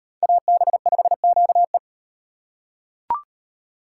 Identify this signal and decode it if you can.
Morse code, heard as a keyed tone, about 35 wpm, 700 Hz. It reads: AB5QE